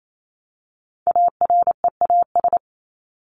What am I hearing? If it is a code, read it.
Morse code, heard as a keyed tone, about 28 wpm, 700 Hz. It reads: AREAS
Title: ES:S Suffix - Top 100 Words with Base Words 28wpm